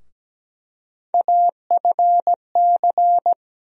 Morse code, heard as a keyed tone, about 17 wpm, 700 Hz. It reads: AFC
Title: USA Call Sign Postfixes - 3 Letters 17wpm